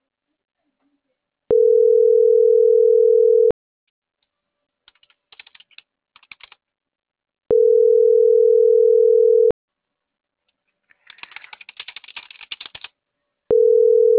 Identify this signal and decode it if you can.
Ringback tone (North America), 440 + 480 Hz.